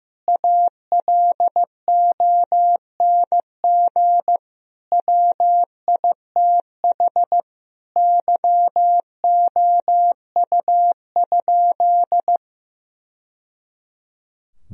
Morse code, heard as a keyed tone, about 15 wpm, 700 Hz. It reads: ALONG WITH YOU?
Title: Sentences from Top 1000 Words - Review Part 3 - 15wpm